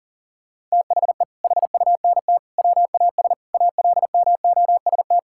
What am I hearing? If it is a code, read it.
Morse code, heard as a keyed tone, about 40 wpm, 700 Hz. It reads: THE HUNT WAS ALMOST
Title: Sentences from Top 1600 Words 40wpm